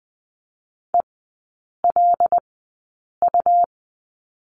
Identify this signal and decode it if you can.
Morse code, heard as a keyed tone, about 20 wpm, spaced as for 8 wpm, 700 Hz. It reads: ELU
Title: 184 Lesson 30e - Letter b - Three Characters 20wpm